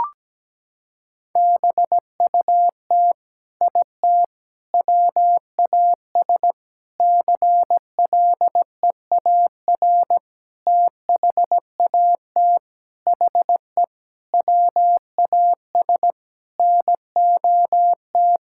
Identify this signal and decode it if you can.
Morse code, heard as a keyed tone, about 17 wpm, 700 Hz. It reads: BUT IT WAS CLEAR THAT HE WAS NOT
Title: Sentences from Top 1600 Words 17wpm